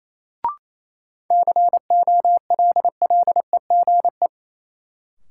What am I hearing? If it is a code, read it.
Morse code, heard as a keyed tone, about 27 wpm, 700 Hz. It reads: COLLEGE
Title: Sentences from Top 1300 Words - New Word Drill Alt - 28wpm